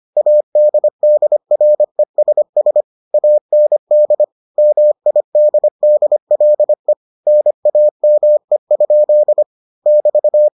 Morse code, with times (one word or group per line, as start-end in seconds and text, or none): 0.2-2.8 ADDRESS
3.1-4.2 AND
4.6-6.9 MIDDLE
7.3-9.4 NAME?
9.9-10.5 <BT>